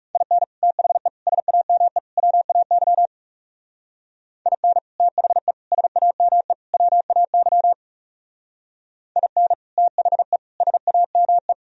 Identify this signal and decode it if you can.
Morse code, heard as a keyed tone, about 39 wpm, 700 Hz. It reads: IN THE SAME WAY IN THE SAME WAY IN THE SAME